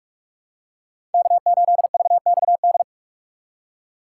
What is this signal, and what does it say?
Morse code, heard as a keyed tone, about 46 wpm, 700 Hz. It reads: K8VXD
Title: USA Call Signs - 5 Characters - 4x Character Spacing - 45wpm